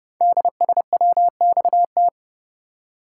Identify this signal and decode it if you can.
Morse code, heard as a keyed tone, about 30 wpm, 700 Hz. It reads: DSWXT
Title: Code Groups - 5 Letters - 30wpm